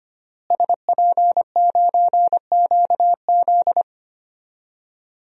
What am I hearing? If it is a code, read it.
Morse code, heard as a keyed tone, about 25 wpm, 700 Hz. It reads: SP9QZ
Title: DXCC-100 Call Signs - 5 Characters - Repeated 3x - 25wpm